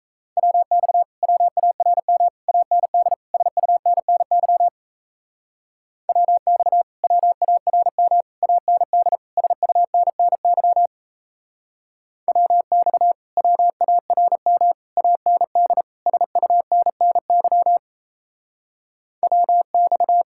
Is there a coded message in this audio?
Yes: Morse code from 0.4 to 20.2 s, heard as a keyed tone, about 38 wpm, 700 Hz. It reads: WX WARM AND SUNNY WX WARM AND SUNNY WX WARM AND SUNNY WX